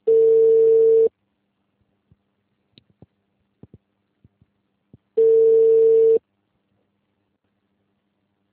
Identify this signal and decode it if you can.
Ringback tone (China, 450 Hz).